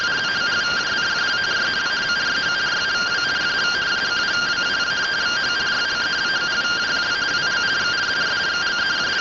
Two-tone teleprinter signal, about 170 Hz shift.